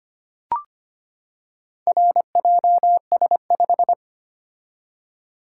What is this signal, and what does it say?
Morse code, heard as a keyed tone, about 25 wpm, 700 Hz. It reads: RJS5
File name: practice / 25wpm / Four Letter-Number Combinations - 25wpm